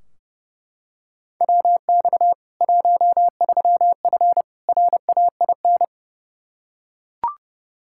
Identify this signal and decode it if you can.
Morse code, heard as a keyed tone, about 29 wpm, 700 Hz. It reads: WX 13F RAIN
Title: K1USN SST QSO Practice 30wpm